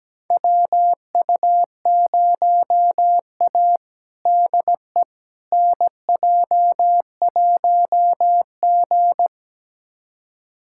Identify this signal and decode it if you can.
Morse code, heard as a keyed tone, about 17 wpm, 700 Hz. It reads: WU0A DE NJ1G
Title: USA Callsign DE Callsign - 4 Characters - 17wpm